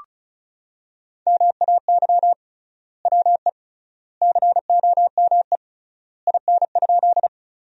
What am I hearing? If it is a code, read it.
Morse code, heard as a keyed tone, about 36 wpm, 700 Hz. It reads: MAY WE COME IN?
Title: 3x Word-Spacing - Sentences from Top 100 Words - 35wpm